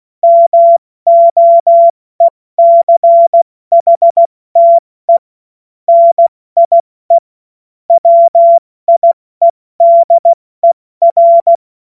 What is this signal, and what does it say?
Morse code, heard as a keyed tone, about 16 wpm, spaced as for 14 wpm, 680 Hz. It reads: MOECHTE NIE WIEDER